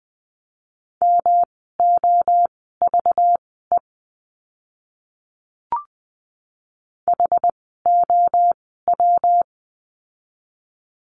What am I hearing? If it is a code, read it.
Morse code, heard as a keyed tone, about 20 wpm, spaced as for 14 wpm, 700 Hz. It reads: MOVE HOW
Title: 237 Lesson 38f - Letter q - Words 20wpm